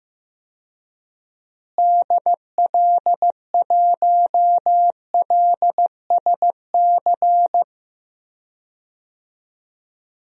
Morse code, heard as a keyed tone, about 16 wpm, 700 Hz. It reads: DL1LSC
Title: DXCC-100 Call Signs - 6 Characters - Repeated 2x - 15wpm